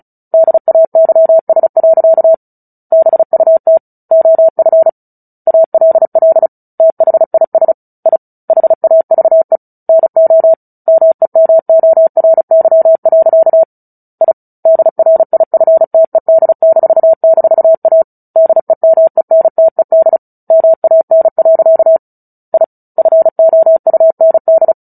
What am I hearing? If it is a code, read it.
Morse code, heard as a keyed tone, about 34 wpm, 650 Hz. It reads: DAYS. BUT OF ALL THIS I HAVE NO MEMORY. I DRIFTED--A DEMENTED MAN. I FOUND
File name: Book Two / 09-Chapter 9 - 35 WPM